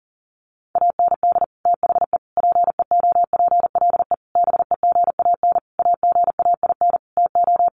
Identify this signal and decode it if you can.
Morse code, heard as a keyed tone, about 40 wpm, 700 Hz. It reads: AND THE PEOPLE BEGAN AGAIN TO
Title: Sentences from Top 1500 Words 40wpm